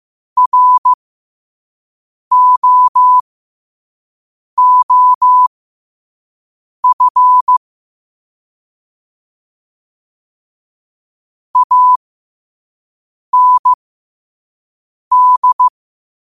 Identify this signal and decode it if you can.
Morse code, heard as a keyed tone, about 15 wpm, spaced as for 5 wpm, 1000 Hz. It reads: ROOF AND